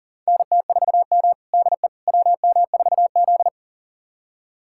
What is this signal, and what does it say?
Morse code, heard as a keyed tone, about 40 wpm, 700 Hz. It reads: NT4M DE WM4Z
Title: USA Callsign DE Callsign - 4 Characters - 40wpm